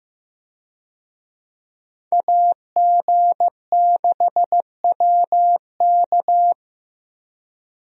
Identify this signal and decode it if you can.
Morse code, heard as a keyed tone, about 15 wpm, 700 Hz. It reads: AG6WK
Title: USA Call Signs - 5 Characters - Repeated 3x - 15wpm